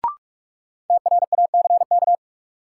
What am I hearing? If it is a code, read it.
Morse code, heard as a keyed tone, about 42 wpm, 700 Hz. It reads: TRACK
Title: Sentences from Top 1100 Words - New Word Drill Alt - 45wpm